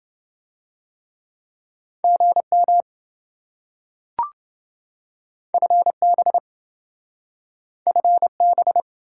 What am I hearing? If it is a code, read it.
Morse code, heard as a keyed tone, about 27 wpm, 700 Hz. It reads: GM FB FB